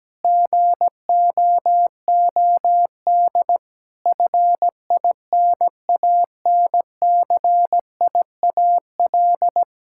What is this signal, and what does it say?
Morse code, heard as a keyed tone, about 17 wpm, 700 Hz. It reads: GOOD FINANCIAL
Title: Sentences from Top 1000 Words - Review Part 5 - 17wpm